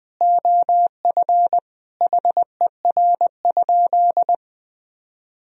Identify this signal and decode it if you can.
Morse code, heard as a keyed tone, about 20 wpm, 700 Hz. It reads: OF HER?
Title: Sentences from Top 200 Words and Abbreviations 20wpm